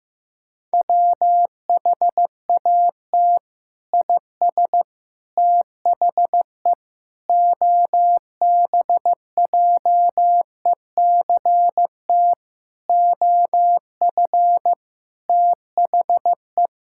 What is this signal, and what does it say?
Morse code, heard as a keyed tone, about 15 wpm, 700 Hz. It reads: WHAT IS THE OBJECT OF THE
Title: Sentences from Top 1000 Words - Review Part 5 - 15wpm